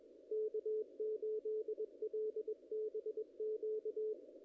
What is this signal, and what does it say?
Morse code, heard as a keyed tone, about 23 wpm, 440 Hz. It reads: K8LBQ
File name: contests / ss-cw-k5tr-op-n5rz -0000